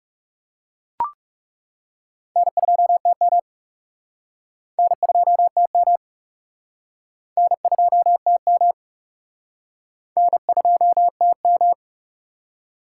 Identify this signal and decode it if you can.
Morse code, heard as a keyed tone, about 40 wpm, 700 Hz. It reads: N2TM N2TM N2TM N2TM